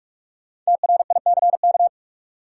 Morse code, heard as a keyed tone, about 46 wpm, 700 Hz. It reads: TRICK
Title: Sentences from Top 1500 Words - New Word Drill Alt - 45wpm